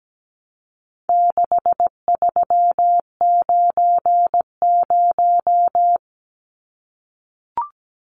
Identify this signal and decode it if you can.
Morse code, heard as a keyed tone, about 17 wpm, 700 Hz. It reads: 6390
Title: Numbers - 4-Digit 17wpm